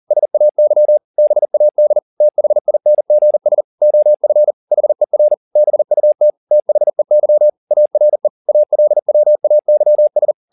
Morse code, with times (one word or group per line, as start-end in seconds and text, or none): 0.1-1.0 SAY
1.2-2.0 BAD
2.2-3.6 THINGS
3.8-4.5 OF
4.7-5.3 HER
5.6-6.3 BUT
6.5-7.5 THEY
7.7-8.3 ARE
8.5-10.3 ALWAYS